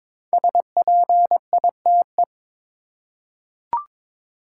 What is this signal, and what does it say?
Morse code, heard as a keyed tone, about 22 wpm, 700 Hz. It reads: SPITE
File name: practice / 22wpm / 5 Letter Words - First 3 Letters Given - 22wpm